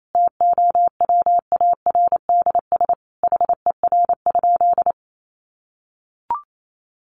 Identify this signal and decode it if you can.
Morse code, heard as a keyed tone, about 28 wpm, 700 Hz. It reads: TOWARDS HER?
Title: Sentences from Top 1100 Words 28wpm